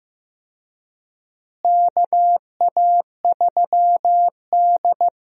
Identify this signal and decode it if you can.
Morse code, heard as a keyed tone, about 15 wpm, 700 Hz. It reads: KA3D
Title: USA Call Signs - 4 Characters - Repeated 2x - 15wpm